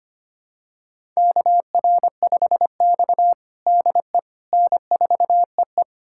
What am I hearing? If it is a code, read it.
Morse code, heard as a keyed tone, about 25 wpm, 700 Hz. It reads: KR5X DE N4EE
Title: USA Callsign DE Callsign - 4 Characters - 25wpm